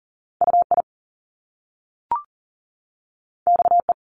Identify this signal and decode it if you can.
Morse code, heard as a keyed tone, about 39 wpm, 700 Hz. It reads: UI XE
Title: Top Two Letter Combinations in Words 40wpm